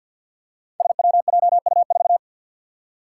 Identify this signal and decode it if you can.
Morse code, heard as a keyed tone, about 52 wpm, 700 Hz. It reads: SWJU4
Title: Five Challenging Letter-Number Combinations 50wpm